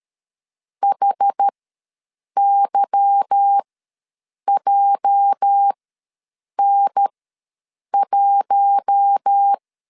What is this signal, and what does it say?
Morse code, heard as a keyed tone, about 13 wpm, spaced as for 7 wpm, 800 Hz. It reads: HYJN1
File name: LEZIONI / 8 YJ45 / 8 YJ4 ws 12wpm 3249